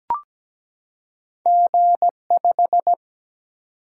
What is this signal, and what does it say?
Morse code, heard as a keyed tone, about 17 wpm, 700 Hz. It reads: G5
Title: Single-Letter Single-Number 17wpm